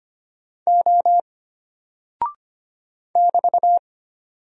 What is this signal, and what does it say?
Morse code, heard as a keyed tone, about 25 wpm, 700 Hz. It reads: O <BT>
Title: Single Character (Letter:Number:Punctuation:Prosign) Alt 25wpm